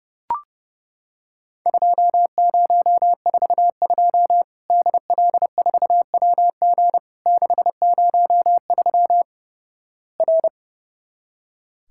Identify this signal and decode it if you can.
Morse code, heard as a keyed tone, about 30 wpm, 700 Hz. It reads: 2042 DL4WG 603 R
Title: Worked All Europe DX Contest - QTC Exchange - 30wpm